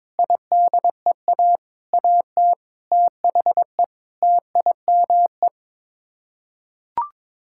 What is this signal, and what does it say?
Morse code, heard as a keyed tone, about 22 wpm, 700 Hz. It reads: IDEA AT THE TIME
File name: Sets of 9 Words - 22wpm